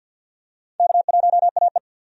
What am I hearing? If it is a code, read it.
Morse code, heard as a keyed tone, about 49 wpm, 700 Hz. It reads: K1AE
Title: USA Call Signs - 4 Characters - 50wpm